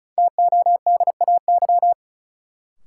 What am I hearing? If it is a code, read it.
Morse code, heard as a keyed tone, about 36 wpm, 700 Hz. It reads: TODAY
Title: Sentences from Top 600 Words - New Word Drill - 35wpm